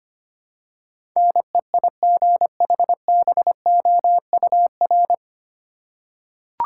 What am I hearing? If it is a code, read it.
Morse code, heard as a keyed tone, about 25 wpm, 700 Hz. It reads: NEIGHBOUR
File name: Nine Letter Words - 25wpm